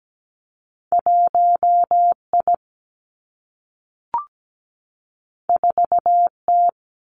Morse code, heard as a keyed tone, about 17 wpm, 700 Hz. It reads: 1I 4T
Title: Two Challenging Letter-Number Combinations 17wpm